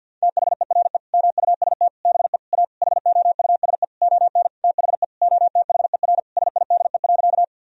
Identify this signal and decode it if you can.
Morse code, heard as a keyed tone, about 46 wpm, 700 Hz. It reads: THERE MUST BE A HOUSE ON THE OTHER SIDE.
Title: Sentences from Top 1000 Words - Review Part 1 - 50wpm